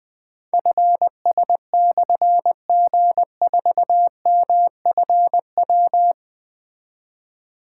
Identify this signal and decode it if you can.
Morse code, heard as a keyed tone, about 20 wpm, 700 Hz. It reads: FS/G4MFW
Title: Difficult Call Signs 20wpm